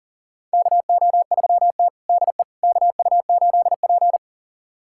Morse code, heard as a keyed tone, about 40 wpm, 700 Hz. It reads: KO3T DE KU8P
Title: USA Callsign DE Callsign - 4 Characters - 40wpm